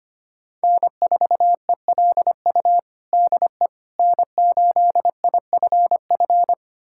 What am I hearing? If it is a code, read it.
Morse code, heard as a keyed tone, about 25 wpm, 700 Hz. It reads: N4ELU DE N8IFF
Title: USA Callsign DE Callsign - 5 Characters - 25wpm